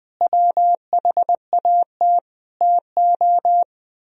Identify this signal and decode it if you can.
Morse code, heard as a keyed tone, about 20 wpm, 700 Hz. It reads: WHAT TO
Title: Sentences from Top 1000 Words - Review Part 2 - 20wpm